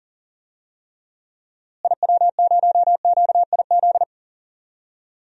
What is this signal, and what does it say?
Morse code, heard as a keyed tone, about 37 wpm, 700 Hz. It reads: IW0QIZ
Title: DXCC-100 Call Signs - 6 Characters - Repeated 3x - 40wpm